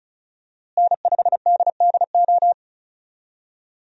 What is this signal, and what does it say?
Morse code, heard as a keyed tone, about 34 wpm, 700 Hz. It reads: N5DDO
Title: USA Call Signs - 5 Characters - Repeated 3x - 35wpm